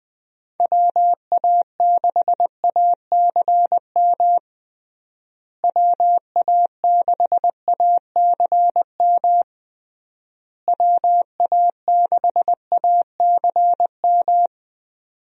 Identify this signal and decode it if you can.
Morse code, heard as a keyed tone, about 20 wpm, 700 Hz. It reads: WA6ACM WA6ACM WA6ACM
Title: USA Call Signs - 6 Characters - Repeated 3x - 20wpm